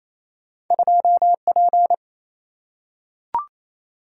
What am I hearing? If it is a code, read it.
Morse code, heard as a keyed tone, about 28 wpm, 700 Hz. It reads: 2P
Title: Single-Letter Single-Number 28wpm